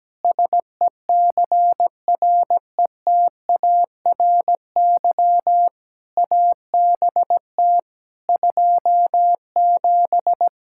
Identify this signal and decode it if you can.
Morse code, heard as a keyed tone, about 17 wpm, 700 Hz. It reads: SECRETARY ABT 27
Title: Practice QSO - Exchange 3 - More Info - 17wpm